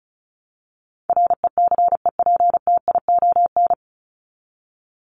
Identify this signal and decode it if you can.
Morse code, heard as a keyed tone, about 34 wpm, 700 Hz. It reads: RECEPTION
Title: Top 1751-2000 Words 35wpm